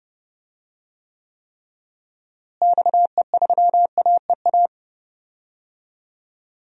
Morse code, heard as a keyed tone, about 30 wpm, 700 Hz. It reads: XE3AEA
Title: DXCC-100 Call Signs - 6 Characters - Repeated 3x - 30wpm